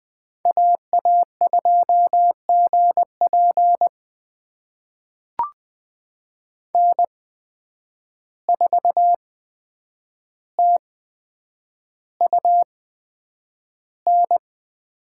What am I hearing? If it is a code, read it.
Morse code, heard as a keyed tone, about 20 wpm, spaced as for 9 wpm, 700 Hz. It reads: AA2GP N4TUN
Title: USA Call Signs - 5 Characters - 8x Character Spacing - 20wpm